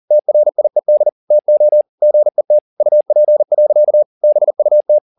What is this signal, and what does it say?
Morse code, heard as a keyed tone, about 40 wpm, 600 Hz. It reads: TRIED TO GET UP. BUT
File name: the old man and the sea 2 40wpm